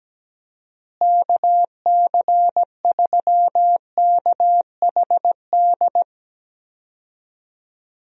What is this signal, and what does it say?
Morse code, heard as a keyed tone, about 17 wpm, 700 Hz. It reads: KC3KHD
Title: USA Call Signs - 6 Characters - Repeated 2x - 17wpm